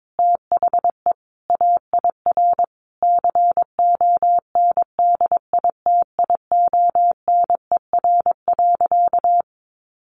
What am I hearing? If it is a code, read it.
Morse code, heard as a keyed tone, about 22 wpm, 700 Hz. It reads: THE AIR CONDITIONER.
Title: Additional Easy Phrases 22wpm